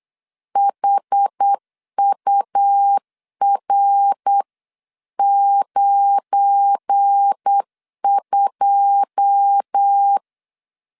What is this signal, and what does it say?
Morse code, heard as a keyed tone, about 8 wpm, 800 Hz. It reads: HUR92
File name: CW / DET / 1 9 1500C S B 8wpm 5624